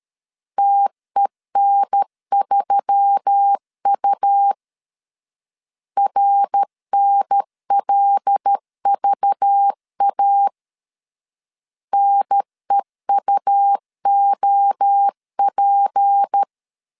Morse code, heard as a keyed tone, about 13 wpm, 800 Hz. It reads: TEN3U RNLVA NEUOP
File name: CW / DET / 1 6 1500C S B 12wpm 3431